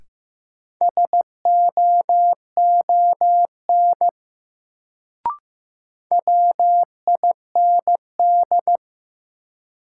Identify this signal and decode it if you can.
Morse code, heard as a keyed tone, about 15 wpm, 700 Hz. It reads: SOON WIND
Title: Sentences from Top 400 Words - New Word Drill - 15wpm